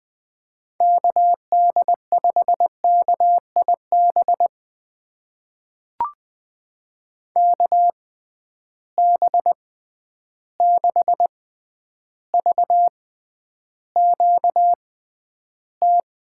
Morse code, heard as a keyed tone, about 20 wpm, spaced as for 10 wpm, 700 Hz. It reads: KD5KIB KB6VQT
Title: USA Call Signs - 6 Characters - 6x Character Spacing - 20wpm